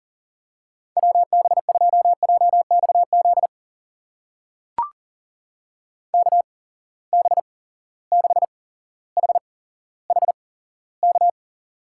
Morse code, heard as a keyed tone, about 40 wpm, spaced as for 17 wpm, 700 Hz. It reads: WB2JXZ KB6HHK
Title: USA Call Signs - 6 Characters - 8x Character Spacing - 40wpm